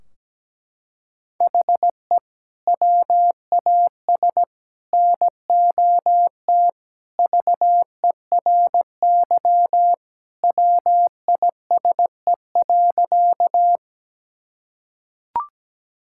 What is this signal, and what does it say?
Morse code, heard as a keyed tone, about 17 wpm, 700 Hz. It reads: HE WAS NOT VERY WISE.
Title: Sentences from Top 1600 Words 17wpm